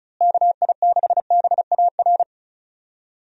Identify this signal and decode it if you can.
Morse code, heard as a keyed tone, about 36 wpm, 700 Hz. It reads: KI6BAR
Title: USA Call Signs - 6 Characters - Repeated 2x - 35wpm